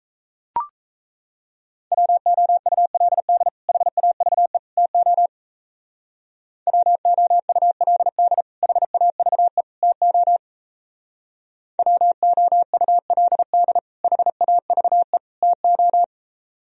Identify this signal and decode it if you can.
Morse code, heard as a keyed tone, about 33 wpm, 700 Hz. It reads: WOULD HAVE TO WOULD HAVE TO WOULD HAVE TO
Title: Speed-Racing - Sets of 3 Words 28wpm